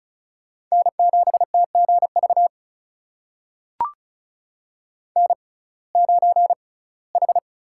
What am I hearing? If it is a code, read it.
Morse code, heard as a keyed tone, about 36 wpm, spaced as for 21 wpm, 700 Hz. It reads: N7TGV N9H
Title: USA Call Signs - 5 Characters - 6x Character Spacing - 35wpm